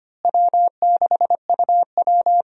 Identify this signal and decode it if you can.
Morse code, heard as a keyed tone, about 25 wpm, 700 Hz. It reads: W6UW
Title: USA Call Signs - 4 Characters - Repeated 2x - 25wpm